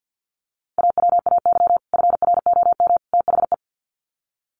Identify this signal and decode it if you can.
Morse code, heard as a keyed tone, about 48 wpm, 700 Hz. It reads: AWAY FROM THE
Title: Sets of 3 Words 50wpm